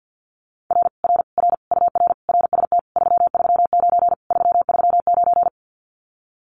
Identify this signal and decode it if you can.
Morse code, heard as a keyed tone, about 50 wpm, 700 Hz. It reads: R R R UR RST 339 339
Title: Common QSO Phrases Encore 50wpm